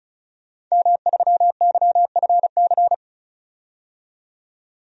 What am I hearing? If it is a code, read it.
Morse code, heard as a keyed tone, about 35 wpm, 700 Hz. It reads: M3YFC
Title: DXCC-100 Call Signs - 5 Characters - 35wpm